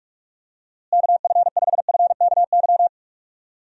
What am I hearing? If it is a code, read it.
Morse code, heard as a keyed tone, about 46 wpm, 700 Hz. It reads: KF5FKY
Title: USA Call Signs - 6 Characters - Repeated 2x - 45wpm